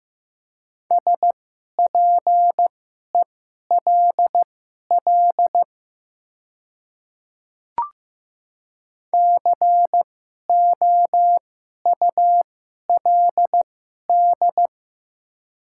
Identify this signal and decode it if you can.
Morse code, heard as a keyed tone, about 15 wpm, spaced as for 11 wpm, 700 Hz. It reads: SPELL COULD
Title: 2x Character Spacing - Top 200 Words - 15wpm